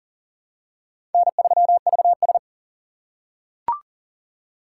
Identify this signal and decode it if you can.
Morse code, heard as a keyed tone, about 40 wpm, 700 Hz. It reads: N3VS
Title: USA Call Signs - 4 Characters - 2x Character Spacing - 40wpm